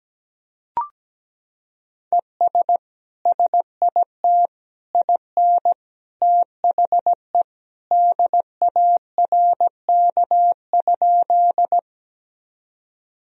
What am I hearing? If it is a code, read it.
Morse code, heard as a keyed tone, about 17 wpm, 700 Hz. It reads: ES SIT IN THE DARK?
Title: Sentences from Top 500 Words and Abbreviations 17wpm